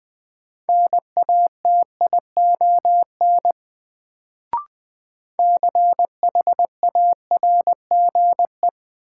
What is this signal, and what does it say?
Morse code, heard as a keyed tone, about 20 wpm, 700 Hz. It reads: NATION CHARGE
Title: Sentences from Top 900 Words - New Word Drill - 20wpm